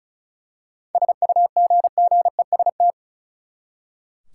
Morse code, heard as a keyed tone, about 36 wpm, 700 Hz. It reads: SUGGEST
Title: Seven Letter Words - 35wpm